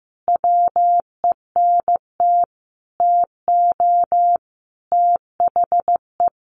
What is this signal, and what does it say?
Morse code, heard as a keyed tone, about 15 wpm, 700 Hz. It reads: WENT TO THE
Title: Sentences from Top 1500 Words 15wpm